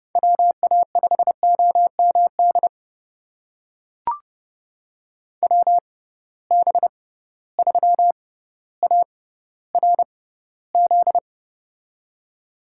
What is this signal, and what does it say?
Morse code, heard as a keyed tone, about 30 wpm, spaced as for 15 wpm, 700 Hz. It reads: WA5OMD WB3ARZ
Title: USA Call Signs - 6 Characters - 6x Character Spacing - 30wpm